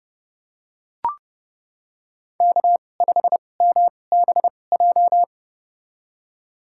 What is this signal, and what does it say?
Morse code, heard as a keyed tone, about 30 wpm, spaced as for 22 wpm, 700 Hz. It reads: K5MBJ
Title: USA Call Signs - 5 Characters - 2x Character Spacing - 30wpm